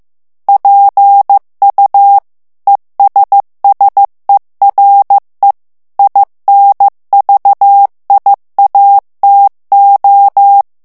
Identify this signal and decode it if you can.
Morse code, heard as a keyed tone, about 15 wpm, 800 Hz. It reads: PU ESSERE INVIATO
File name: ITALIEN texte clair 1 vitesse 14 mots minute